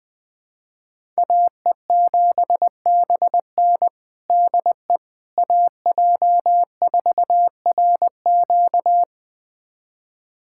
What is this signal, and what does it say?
Morse code, heard as a keyed tone, about 21 wpm, 700 Hz. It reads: AE7BN DE AJ4RQ
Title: USA Callsign DE Callsign - 5 Characters - 20wpm